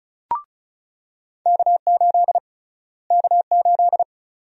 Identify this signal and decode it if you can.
Morse code, heard as a keyed tone, about 34 wpm, 700 Hz. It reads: K8 K8
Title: USA Call Sign Prefixes - Repeated 2x - 35wpm